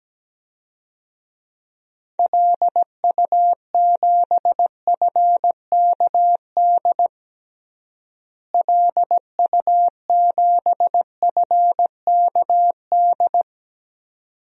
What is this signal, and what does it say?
Morse code, heard as a keyed tone, about 17 wpm, 700 Hz. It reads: LU7FKD LU7FKD
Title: DXCC-100 Call Signs - 6 Characters - Repeated 2x - 17wpm